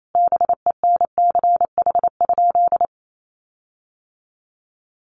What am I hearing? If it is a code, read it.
Morse code, heard as a keyed tone, about 28 wpm, 700 Hz. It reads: BENCH?
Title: Sentences from Top 1400 Words 28wpm